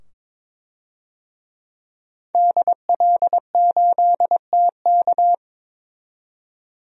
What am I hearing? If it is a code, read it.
Morse code, heard as a keyed tone, about 22 wpm, 700 Hz. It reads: DL8TK
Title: DXCC-100 Call Signs - 5 Characters - 22wpm